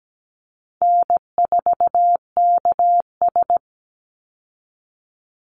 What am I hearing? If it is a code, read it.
Morse code, heard as a keyed tone, about 17 wpm, 700 Hz. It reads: N4KS
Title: Easy Call Signs Repeated 2x - 17wpm